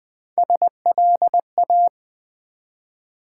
Morse code, heard as a keyed tone, about 20 wpm, 700 Hz. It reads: SLA
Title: Build A Word - 1 Letter At A Time - 4-Letter Words - 20wpm